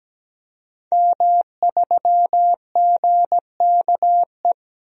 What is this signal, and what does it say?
Morse code, heard as a keyed tone, about 17 wpm, 700 Hz. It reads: M3GKE
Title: DXCC-100 Call Signs - 5 Characters - 17wpm